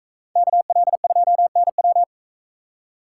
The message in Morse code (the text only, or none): KL2NW